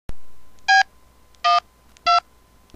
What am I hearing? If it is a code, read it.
DTMF, heard as telephone touch tones, tones of about 140 ms. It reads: B 1 2